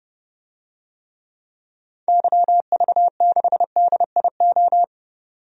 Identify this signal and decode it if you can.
Morse code, heard as a keyed tone, about 32 wpm, 700 Hz. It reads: YV6DIO
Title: DXCC-100 Call Signs - 6 Characters - 30wpm